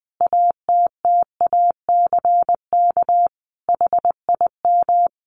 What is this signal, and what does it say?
Morse code, heard as a keyed tone, about 20 wpm, 700 Hz. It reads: ATTACK HIM
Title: Sentences from Top 1500 Words 20wpm